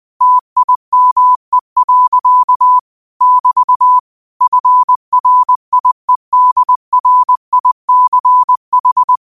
Morse code, heard as a keyed tone, about 20 wpm, 1000 Hz. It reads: TIME. <BT> FRIEDRICH